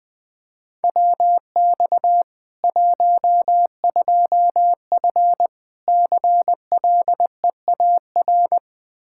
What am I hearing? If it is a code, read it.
Morse code, heard as a keyed tone, about 20 wpm, 700 Hz. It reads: WX 12F CLEAR
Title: Practice QSO - Exchange 2 - Info - 20wpm